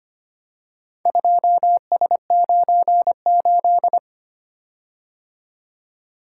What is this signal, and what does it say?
Morse code, heard as a keyed tone, about 25 wpm, 700 Hz. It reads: 2S98
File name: practice / 25wpm / Four Letter-Number Combinations - 25wpm